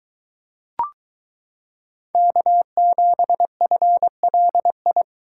Morse code, heard as a keyed tone, about 22 wpm, 700 Hz. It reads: K7FLI